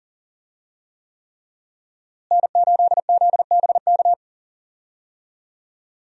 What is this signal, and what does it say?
Morse code, heard as a keyed tone, about 40 wpm, 700 Hz. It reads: N8ZBK
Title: USA Call Signs - 5 Characters - Repeated 3x - 40wpm